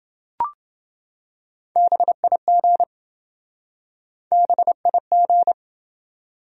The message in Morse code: BIG BIG

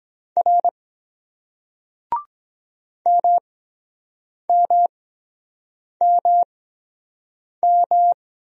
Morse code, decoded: R MMMM

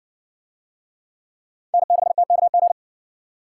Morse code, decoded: N6TKG

